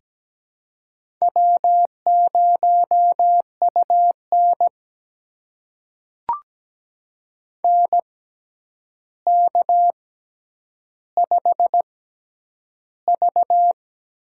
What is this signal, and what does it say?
Morse code, heard as a keyed tone, about 17 wpm, spaced as for 9 wpm, 700 Hz. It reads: W0UN NK5V
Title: USA Call Signs - 4 Characters - 6x Character Spacing - 17wpm